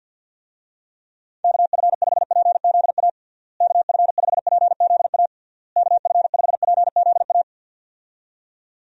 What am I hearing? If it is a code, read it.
Morse code, heard as a keyed tone, about 50 wpm, 700 Hz. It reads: KF5PZA KF5PZA KF5PZA
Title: USA Call Signs - 6 Characters - Repeated 3x - 50wpm